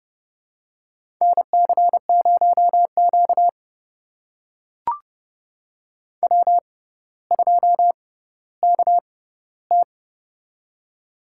Morse code, heard as a keyed tone, about 30 wpm, spaced as for 15 wpm, 700 Hz. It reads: NC0Q W2KT
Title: USA Call Signs - 4 Characters - 6x Character Spacing - 30wpm